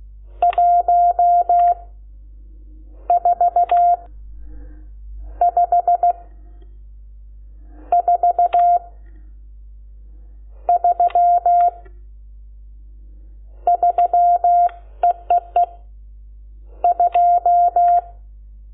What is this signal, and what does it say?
Morse code, heard as a keyed tone, about 16 wpm, 700 Hz. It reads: 1 4 5 4 3 3 EEE 2
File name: Lecciones / CLASE Nº12